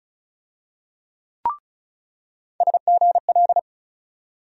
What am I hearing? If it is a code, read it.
Morse code, heard as a keyed tone, about 36 wpm, 700 Hz. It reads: SGL
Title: USA Call Sign Postfixes - 3 Letters 35wpm